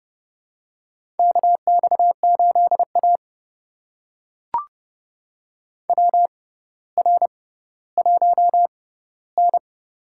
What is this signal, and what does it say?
Morse code, heard as a keyed tone, about 30 wpm, spaced as for 15 wpm, 700 Hz. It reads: KX8A WR1N
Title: USA Call Signs - 4 Characters - 6x Character Spacing - 30wpm